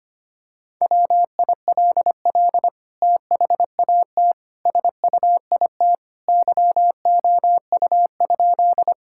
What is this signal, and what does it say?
Morse code, heard as a keyed tone, about 25 wpm, 700 Hz. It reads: WILL THAT SUIT YOU?
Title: Sentences from Top 1600 Words 25wpm